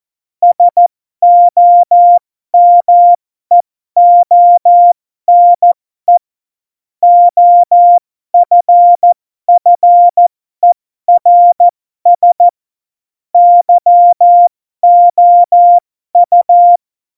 Morse code, heard as a keyed tone, about 14 wpm, spaced as for 12 wpm, 700 Hz. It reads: SOMEONE OFFERS YOU
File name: Morsekurs / Witze